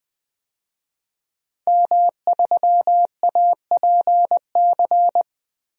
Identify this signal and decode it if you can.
Morse code, heard as a keyed tone, about 21 wpm, 700 Hz. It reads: M3APC
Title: DXCC-100 Call Signs - 5 Characters - Repeated 3x - 20wpm